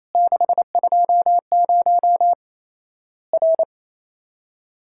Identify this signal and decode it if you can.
Morse code, heard as a keyed tone, about 28 wpm, 700 Hz. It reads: 620 R
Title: Worked All Europe DX Contest - QTC Exchange - 28wpm